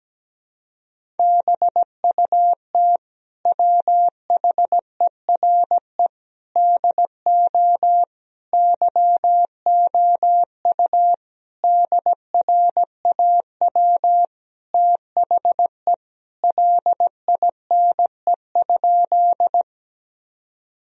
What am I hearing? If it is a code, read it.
Morse code, heard as a keyed tone, about 17 wpm, 700 Hz. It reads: BUT WHERE DO YOU DRAW THE LINE?
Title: Sentences from Top 1000 Words - Review Part 1 - 17wpm